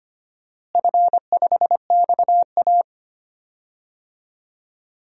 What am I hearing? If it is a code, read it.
Morse code, heard as a keyed tone, about 25 wpm, 700 Hz. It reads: F5XA
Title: DXCC-100 Call Signs - 4 Characters - Repeated 3x - 25wpm